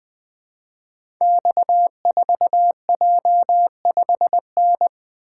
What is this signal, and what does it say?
Morse code, heard as a keyed tone, about 20 wpm, 700 Hz. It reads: X4J5N
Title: Code Groups - 5 Letter:Numbers - 20wpm